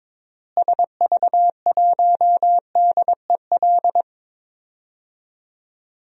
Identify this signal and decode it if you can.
Morse code, heard as a keyed tone, about 22 wpm, 700 Hz. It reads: SV1DEL
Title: DXCC-100 Call Signs - 6 Characters - Repeated 2x - 22wpm